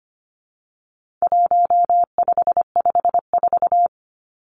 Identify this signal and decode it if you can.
Morse code, heard as a keyed tone, about 25 wpm, 700 Hz. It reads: 1554
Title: Numbers - 4-Digit 25wpm